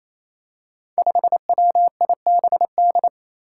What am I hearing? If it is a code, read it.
Morse code, heard as a keyed tone, about 27 wpm, 700 Hz. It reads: 5WIBD
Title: Five Challenging Letter-Number Combinations 28wpm